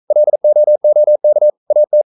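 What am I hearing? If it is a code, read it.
Morse code, heard as a keyed tone, about 42 wpm, 600 Hz. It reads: LOOK AT
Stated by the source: Morse code practice at 41 wpm, from CW med Gnister.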